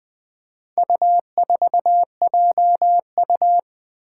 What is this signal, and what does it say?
Morse code, heard as a keyed tone, about 20 wpm, 700 Hz. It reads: U4JU
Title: Four Challenging Letter-Number Combinations 20wpm